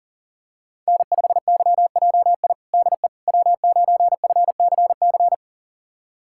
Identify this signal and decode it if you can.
Morse code, heard as a keyed tone, about 37 wpm, 700 Hz. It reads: N5YJI DE W9FCC
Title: USA Callsign DE Callsign - 5 Characters - 40wpm